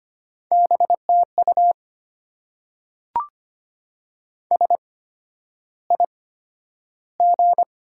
Morse code, heard as a keyed tone, about 25 wpm, spaced as for 11 wpm, 700 Hz. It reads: BTU SIG
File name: Top 90 QSO Elements - 8x Character Spacing - 25wpm